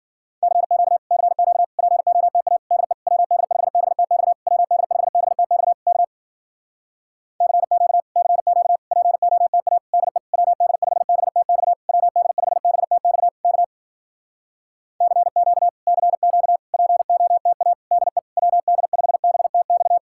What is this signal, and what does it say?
Morse code, heard as a keyed tone, about 53 wpm, 700 Hz. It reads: CQ CQ POTA DE WD5BTX WD5BTX K CQ CQ POTA DE WD5BTX WD5BTX K CQ CQ POTA DE WD5BTX